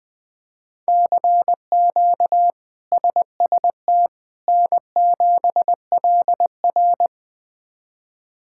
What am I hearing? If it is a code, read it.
Morse code, heard as a keyed tone, about 20 wpm, 700 Hz. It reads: CQ SST N7LR
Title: K1USN SST QSO Practice - 20wpm